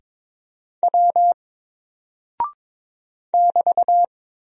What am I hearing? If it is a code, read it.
Morse code, heard as a keyed tone, about 22 wpm, 700 Hz. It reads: W <BT>
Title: Single Character (Letter:Number:Punctuation:Prosign) 22wpm